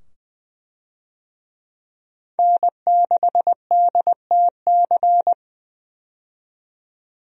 Morse code, heard as a keyed tone, about 20 wpm, 700 Hz. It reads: N6DTC
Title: USA Call Signs - 5 Characters - 20wpm